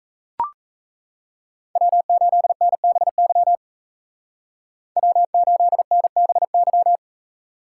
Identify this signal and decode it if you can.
Morse code, heard as a keyed tone, about 35 wpm, 700 Hz. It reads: W8NBY W8NBY